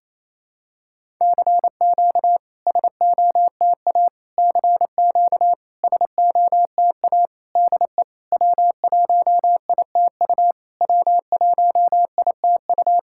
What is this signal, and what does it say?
Morse code, heard as a keyed tone, about 28 wpm, 700 Hz. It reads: CQ SOTA CQ SOTA DE W1ITU W1ITU
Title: SOTA QSO Practice 28wpm